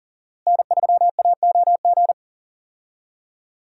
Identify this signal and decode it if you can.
Morse code, heard as a keyed tone, about 39 wpm, 700 Hz. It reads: N3AOG
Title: USA Call Signs - 5 Characters - Repeated 2x - 40wpm